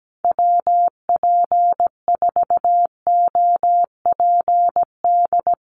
Morse code, heard as a keyed tone, about 17 wpm, 700 Hz. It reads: WP4OPD
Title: USA Call Signs Repeated 2x 17wpm